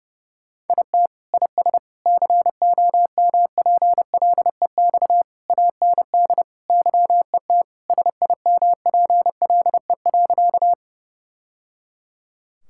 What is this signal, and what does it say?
Morse code, heard as a keyed tone, about 30 wpm, 700 Hz. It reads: IT IS COMPLEX AND YET SIMPLE.
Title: Sentences from Top 1000 Words - Review Part 6 - 30wpm